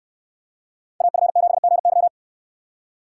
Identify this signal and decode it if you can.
Morse code, heard as a keyed tone, about 69 wpm, 690 Hz. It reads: AF7GY